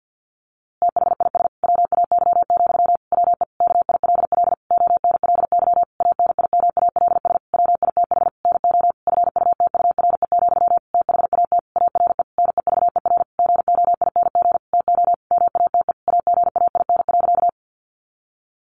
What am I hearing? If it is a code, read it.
Morse code, heard as a keyed tone, about 50 wpm, 700 Hz. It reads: THIS WAY, WE KILL ONLY ANIMALS WITH NO FUTURE, THAT ARE NEVER GOING TO MATE AGAIN.
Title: A Sound Of Thunder 50wpm